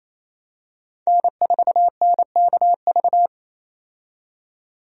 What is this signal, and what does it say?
Morse code, heard as a keyed tone, about 28 wpm, 700 Hz. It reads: N4NKV
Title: USA Call Signs - 5 Characters - 28wpm